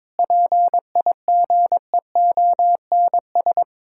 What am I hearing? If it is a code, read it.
Morse code, heard as a keyed tone, about 22 wpm, 700 Hz. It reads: PIGEONS
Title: Pangram Challenge v2 - 22wpm